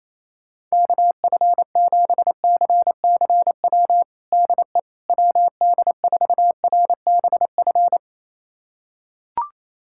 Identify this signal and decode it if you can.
Morse code, heard as a keyed tone, about 28 wpm, 700 Hz. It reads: KF7CCW DE WD4RBF
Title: USA Callsign DE Callsign - 6 Characters - 28wpm